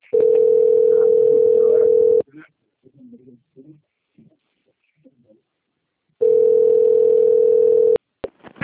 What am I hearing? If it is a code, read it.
Ringback tone (North America), 440 + 480 Hz.
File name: out- - - - - .16109